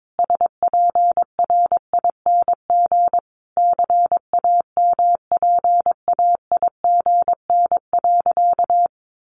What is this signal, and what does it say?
Morse code, heard as a keyed tone, about 22 wpm, 700 Hz. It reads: SPRING CAMPAIGN.
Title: Sentences from Top 1200 Words 22wpm